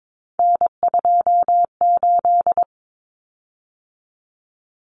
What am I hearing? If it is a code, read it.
Morse code, heard as a keyed tone, about 22 wpm, 700 Hz. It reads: N28
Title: Common Cut Numbers - 3-Digit 22wpm